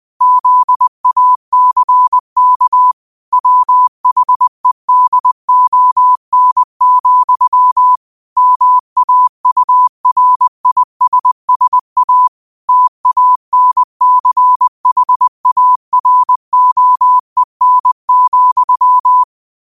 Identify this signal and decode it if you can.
Morse code, heard as a keyed tone, about 20 wpm, 1000 Hz. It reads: ZACK WHEDON, MAURISSA TANCHAROEN,
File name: qotdfiles / 20wpm